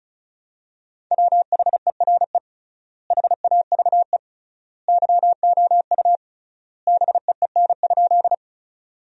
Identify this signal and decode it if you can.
Morse code, heard as a keyed tone, about 34 wpm, 700 Hz. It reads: WHERE HAVE YOU BEEN?
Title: 3x Word-Spacing - Sentences from Top 200 Words - 35wpm